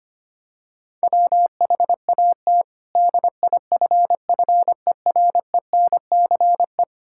Morse code, heard as a keyed tone, about 25 wpm, 700 Hz. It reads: WHAT DIFFERENCE
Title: Sentences from Top 1000 Words - Review Part 5 - 25wpm